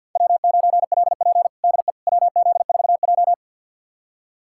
Morse code, heard as a keyed tone, about 51 wpm, 700 Hz. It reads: W9LP DE WZ4J
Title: USA Callsign DE Callsign - 4 Characters - 50wpm